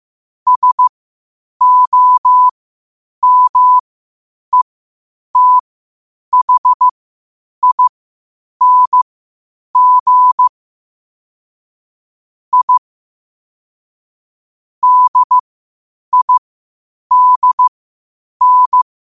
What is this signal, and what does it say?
Morse code, heard as a keyed tone, about 15 wpm, spaced as for 8 wpm, 1000 Hz. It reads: SOMETHING I DIDN
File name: qotdfiles / 8wpm